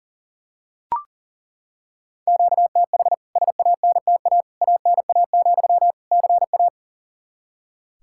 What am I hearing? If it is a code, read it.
Morse code, heard as a keyed tone, about 40 wpm, 700 Hz. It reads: QTH SANTA ANA, CA
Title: K1USN SST QSO Practice 40wpm